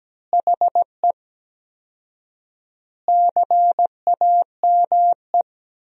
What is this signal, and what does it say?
Morse code, heard as a keyed tone, about 17 wpm, 700 Hz. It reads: HE CAME
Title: 4x Word-Spacing - Sentences from Top 500 Words - 17wpm